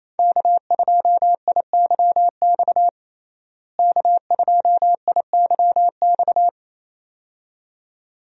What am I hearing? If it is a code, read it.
Morse code, heard as a keyed tone, about 28 wpm, 700 Hz. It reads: K2IYX K2IYX
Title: USA Call Signs - 5 Characters - Repeated 2x - 28wpm